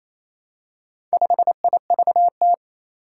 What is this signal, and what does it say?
Morse code, heard as a keyed tone, about 27 wpm, 700 Hz. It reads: 5IVT
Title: Four Challenging Letter-Number Combinations 28wpm